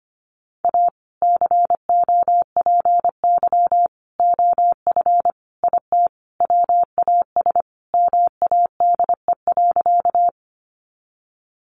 Morse code, heard as a keyed tone, about 25 wpm, 700 Hz. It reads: A COPY OF IT WAS MADE.
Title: Sentences from Top 1400 Words 25wpm